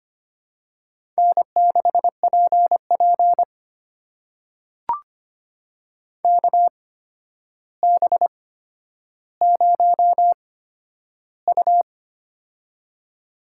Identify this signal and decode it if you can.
Morse code, heard as a keyed tone, about 25 wpm, spaced as for 11 wpm, 700 Hz. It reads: N6PP KB0U
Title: USA Call Signs - 8x Character Spacing - 25wpm